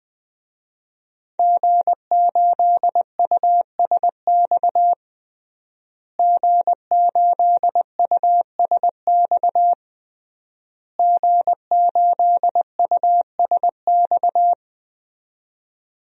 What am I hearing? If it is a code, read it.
Morse code, heard as a keyed tone, about 20 wpm, 700 Hz. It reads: G8USX G8USX G8USX
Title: DXCC-100 Call Signs - 5 Characters - Repeated 3x - 20wpm